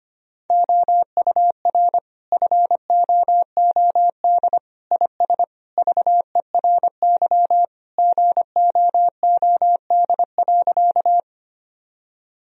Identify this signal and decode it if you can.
Morse code, heard as a keyed tone, about 25 wpm, 700 Hz. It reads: OUR FOOD IS VERY GOOD.
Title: Sentences from Top 1000 Words - Review Part 1 - 25wpm